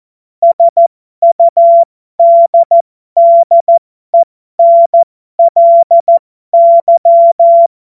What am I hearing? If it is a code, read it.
Morse code, heard as a keyed tone, about 14 wpm, spaced as for 12 wpm, 680 Hz. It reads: SUDDENLY
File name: Morsekurs / Witze